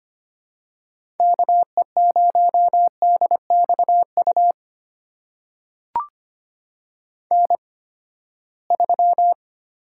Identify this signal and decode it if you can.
Morse code, heard as a keyed tone, about 25 wpm, spaced as for 17 wpm, 700 Hz. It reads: KE0DXU N3
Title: USA Call Signs - 8x Character Spacing - 25wpm